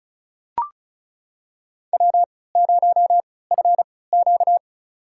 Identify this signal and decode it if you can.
Morse code, heard as a keyed tone, about 34 wpm, spaced as for 20 wpm, 700 Hz. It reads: W0FQ
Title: USA Call Signs - 4 Characters - 3x Character Spacing - 35wpm